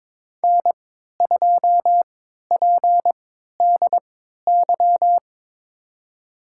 Morse code, heard as a keyed tone, about 22 wpm, spaced as for 13 wpm, 700 Hz. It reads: N2PDY
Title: USA Call Signs - 5 Characters - 3x Character Spacing - 22wpm